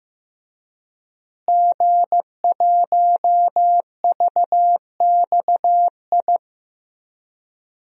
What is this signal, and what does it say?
Morse code, heard as a keyed tone, about 15 wpm, 700 Hz. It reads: G1VXI
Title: DXCC-100 Call Signs - 5 Characters - Repeated 3x - 15wpm